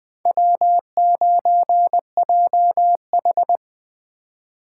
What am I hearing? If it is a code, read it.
Morse code, heard as a keyed tone, about 20 wpm, 700 Hz. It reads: W9JH
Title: Speed Racing - USA 1x2 Call Signs 17wpm